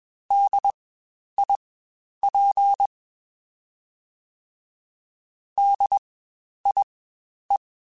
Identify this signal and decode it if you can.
Morse code, heard as a keyed tone, about 21 wpm, spaced as for 10 wpm, 800 Hz. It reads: DIP DIE
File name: Spaced Alphas / Spaced Alphas mp3s / Ds sp @20